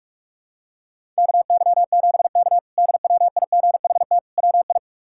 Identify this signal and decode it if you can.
Morse code, heard as a keyed tone, about 46 wpm, 700 Hz. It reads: KY7K DWIGHT WI
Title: K1USN SST QSO Practice - 45wpm